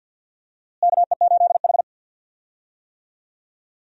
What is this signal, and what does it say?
Morse code, heard as a keyed tone, about 49 wpm, 700 Hz. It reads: KE8H
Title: USA Call Signs - 4 Characters - Repeated 3x - 50wpm